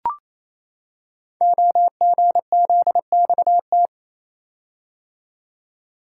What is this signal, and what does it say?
Morse code, heard as a keyed tone, about 29 wpm, 700 Hz. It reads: OGZXT
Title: Code Groups - 5 Letters - 28wpm